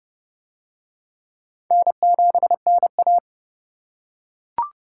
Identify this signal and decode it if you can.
Morse code, heard as a keyed tone, about 29 wpm, 700 Hz. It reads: N7NA